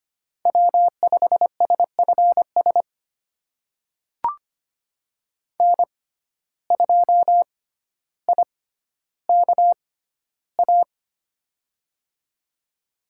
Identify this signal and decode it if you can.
Morse code, heard as a keyed tone, about 25 wpm, spaced as for 13 wpm, 700 Hz. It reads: W5SFS N2IKA
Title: USA Call Signs - 5 Characters - 6x Character Spacing - 25wpm